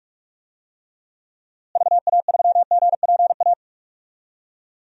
Morse code, heard as a keyed tone, about 47 wpm, 700 Hz. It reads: VA3GPA
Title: DXCC-100 Call Signs - 6 Characters - Repeated 2x - 45wpm